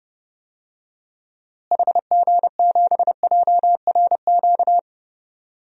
Morse code, heard as a keyed tone, about 30 wpm, 700 Hz. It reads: HG7JRQ
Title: DXCC-100 Call Signs - 6 Characters - Repeated 2x - 30wpm